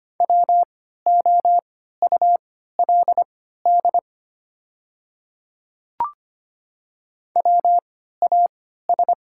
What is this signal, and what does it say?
Morse code, heard as a keyed tone, about 25 wpm, spaced as for 14 wpm, 690 Hz. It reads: WOULD WAS
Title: 3x Character Spacing - Top 100 Words - 25wpm